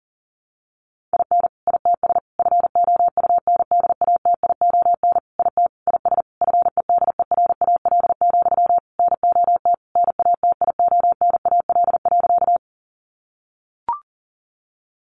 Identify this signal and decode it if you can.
Morse code, heard as a keyed tone, about 40 wpm, 700 Hz. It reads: IN ITS FOUNDATION IT IS FEDERAL, NOT NATIONAL.
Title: Sentences from Top 1200 Words 40wpm